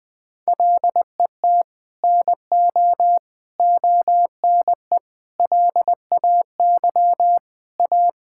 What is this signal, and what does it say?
Morse code, heard as a keyed tone, about 21 wpm, 700 Hz. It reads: LET NO ONE LAY A
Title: Sentences from Top 1600 Words 20wpm